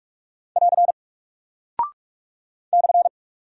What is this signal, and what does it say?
Morse code, heard as a keyed tone, about 46 wpm, 700 Hz. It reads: <AR> /
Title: Single Character (Letter:Number:Punctuation:Prosign) 45wpm